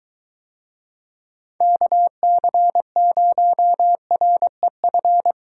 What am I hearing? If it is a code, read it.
Morse code, heard as a keyed tone, about 23 wpm, 700 Hz. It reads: KC0REF